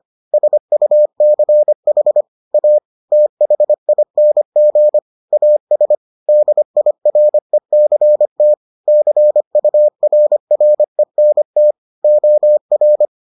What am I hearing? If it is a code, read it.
Morse code, heard as a keyed tone, about 25 wpm, 600 Hz. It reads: SUCH A THING AS DIRECT CURRENT OR